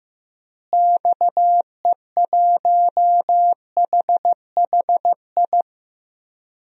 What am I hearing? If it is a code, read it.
Morse code, heard as a keyed tone, about 15 wpm, 700 Hz. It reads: XE1HHI
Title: DXCC-100 Call Signs - 6 Characters - Repeated 3x - 15wpm